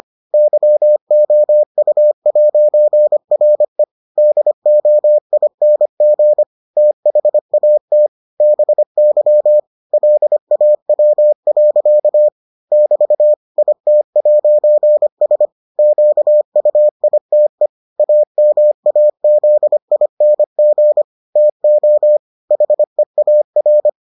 Morse code, heard as a keyed tone, about 25 wpm, 600 Hz. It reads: YOU'RE DOING THAT BY LAW. <BT> IT'S QUITE AMAZING TO HEAR